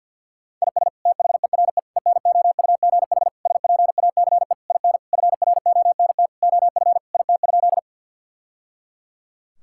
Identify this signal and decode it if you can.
Morse code, heard as a keyed tone, about 50 wpm, 700 Hz. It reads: IS THERE ENOUGH SPACE IN FRONT OF IT?
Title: Sentences from Top 1000 Words - Review Part 5 - 50wpm